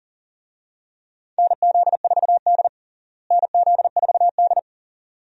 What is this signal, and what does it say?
Morse code, heard as a keyed tone, about 40 wpm, 700 Hz. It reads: NZ4D NZ4D
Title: USA Call Signs - 4 Characters - Repeated 2x - 40wpm